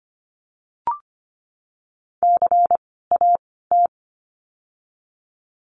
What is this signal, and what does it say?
Morse code, heard as a keyed tone, about 25 wpm, 700 Hz. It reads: C A T
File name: course / 25:10wpm / 094 Lesson 16f - Letter c - Words 25wpm